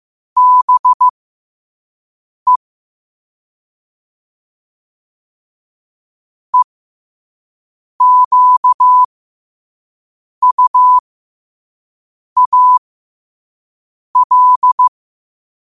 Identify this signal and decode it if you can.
Morse code, heard as a keyed tone, about 15 wpm, spaced as for 5 wpm, 1000 Hz. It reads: BE EQUAL
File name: qotdfiles / 5wpm